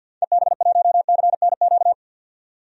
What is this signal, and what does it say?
Morse code, heard as a keyed tone, about 47 wpm, 700 Hz. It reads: EB1CNQ
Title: DXCC-100 Call Signs - 6 Characters - 50wpm